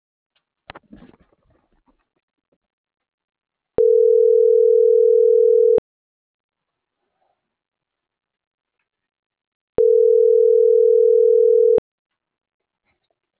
Ringback tone (North America), 440 + 480 Hz.